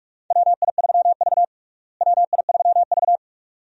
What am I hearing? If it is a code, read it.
Morse code, heard as a keyed tone, about 43 wpm, 700 Hz. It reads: WI3V WI3V
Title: USA Call Signs - 4 Characters - Repeated 2x - 45wpm